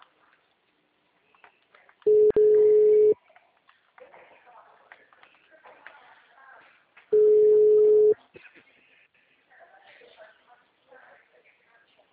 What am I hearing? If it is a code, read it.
Ringback tone (Europe, 425 Hz), with speech.
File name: out- -2003- - - .79888